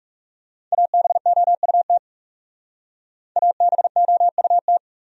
Morse code, heard as a keyed tone, about 36 wpm, 700 Hz. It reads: ABOUT ABOUT
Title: Speed-Racing Top 200 Words 30wpm